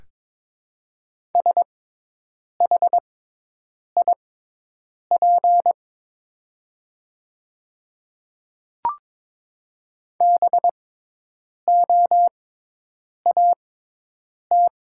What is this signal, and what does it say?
Morse code, heard as a keyed tone, about 22 wpm, spaced as for 8 wpm, 700 Hz. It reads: SHIP BOAT
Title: 6x Character Spacing - Top 500 Words - 22wpm